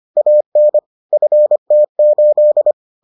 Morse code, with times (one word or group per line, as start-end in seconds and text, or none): 0.2-0.8 AN
1.1-2.7 FT8